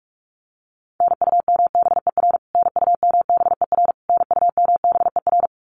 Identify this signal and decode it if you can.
Morse code, heard as a keyed tone, about 44 wpm, 700 Hz. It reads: NUMBER NUMBER NUMBER
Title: Top 500 Words Repeated 45wpm